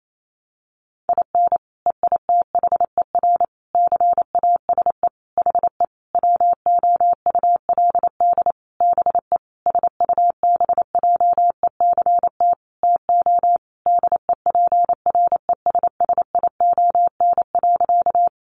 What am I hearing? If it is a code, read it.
Morse code, heard as a keyed tone, about 28 wpm, 700 Hz. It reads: IN EITHER CASE HE WOULD BE SUBJECT TO DEPRESSION.
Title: Sentences from Top 1300 Words 28wpm